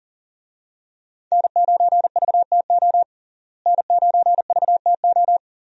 Morse code, heard as a keyed tone, about 37 wpm, 700 Hz. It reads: N9VTO N9VTO
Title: USA Call Signs - 5 Characters - Repeated 3x - 40wpm